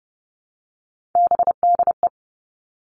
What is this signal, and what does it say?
Morse code, heard as a keyed tone, about 32 wpm, 700 Hz. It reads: BDE
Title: Three Challenging Letter-Number Combinations 30wpm